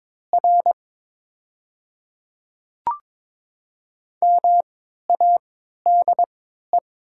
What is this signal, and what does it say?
Morse code, heard as a keyed tone, about 21 wpm, spaced as for 12 wpm, 700 Hz. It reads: R MADE
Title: 3x Character Spacing - Top 200 Words - 22wpm